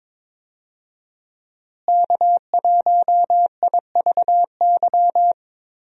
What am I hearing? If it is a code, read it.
Morse code, heard as a keyed tone, about 22 wpm, 700 Hz. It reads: K1IVY
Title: USA Call Signs - 5 Characters - Repeated 2x - 22wpm